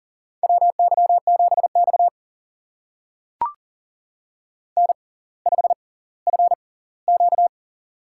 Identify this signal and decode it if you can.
Morse code, heard as a keyed tone, about 39 wpm, spaced as for 20 wpm, 700 Hz. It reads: WY7X N5FQ
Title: USA Call Signs - 4 Characters - 6x Character Spacing - 40wpm